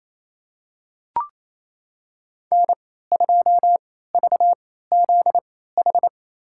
Morse code, heard as a keyed tone, about 28 wpm, spaced as for 16 wpm, 700 Hz. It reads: N2VZH
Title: USA Call Signs - 3x Character Spacing - 28wpm